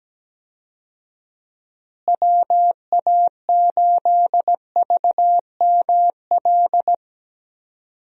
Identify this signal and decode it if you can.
Morse code, heard as a keyed tone, about 17 wpm, 700 Hz. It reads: WA8VML
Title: USA Call Signs - 6 Characters - Repeated 3x - 17wpm